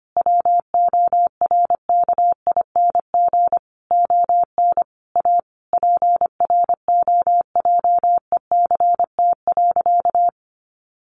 Morse code, heard as a keyed tone, about 25 wpm, 700 Hz. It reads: WORKING ON A PROJECT.
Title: Sentences from Top 1400 Words 25wpm